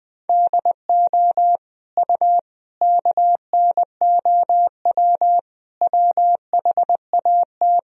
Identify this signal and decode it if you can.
Morse code, heard as a keyed tone, about 20 wpm, 700 Hz. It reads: DO U KNOW WHAT
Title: Sentences from Top 300 Words and Abbreviations 20wpm